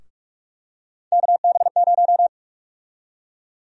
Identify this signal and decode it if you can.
Morse code, heard as a keyed tone, about 46 wpm, 700 Hz. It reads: KB0
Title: USA Call Sign Prefixes - 45wpm